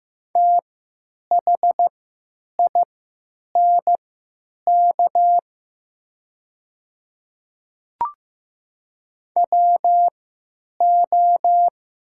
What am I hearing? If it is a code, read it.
Morse code, heard as a keyed tone, about 15 wpm, spaced as for 8 wpm, 700 Hz. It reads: THINK WO
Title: 3x Character Spacing - Top 200 Words - 15wpm